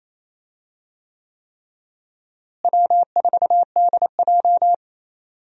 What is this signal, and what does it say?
Morse code, heard as a keyed tone, about 28 wpm, 700 Hz. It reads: W4DJ
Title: USA Call Signs - 4 Characters - Repeated 3x - 28wpm